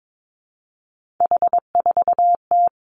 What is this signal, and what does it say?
Morse code, heard as a keyed tone, about 22 wpm, 700 Hz. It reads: H4T
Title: Three Challenging Letter-Number Combinations 22wpm